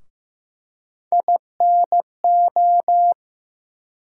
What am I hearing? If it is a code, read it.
Morse code, heard as a keyed tone, about 16 wpm, 700 Hz. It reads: INO
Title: USA Call Sign Postfixes - 3 Letters 15wpm